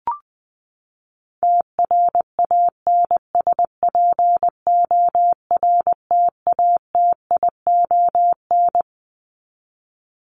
Morse code, heard as a keyed tone, about 20 wpm, 700 Hz. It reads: TRANSPORTATION
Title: Top 1501-1750 Words 20wpm